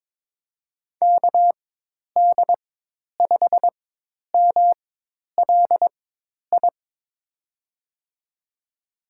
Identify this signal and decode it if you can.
Morse code, heard as a keyed tone, about 22 wpm, spaced as for 10 wpm, 700 Hz. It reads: KD5MLI
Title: USA Call Signs - 4x Character Spacing - 22wpm